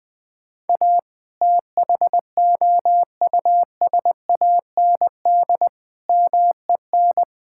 Morse code, heard as a keyed tone, about 21 wpm, 700 Hz. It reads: A THOUSAND MEN
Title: Sentences from Top 1000 Words - Review Part 6 - 20wpm